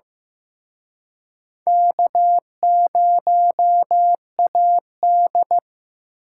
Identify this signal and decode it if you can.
Morse code, heard as a keyed tone, about 18 wpm, 700 Hz. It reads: K0AD
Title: Speed Racing - USA 1x2 Call Signs 15wpm